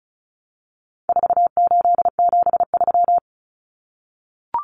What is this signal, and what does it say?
Morse code, heard as a keyed tone, about 34 wpm, 700 Hz. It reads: 4873
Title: Common Cut Numbers - 4-Digit 35wpm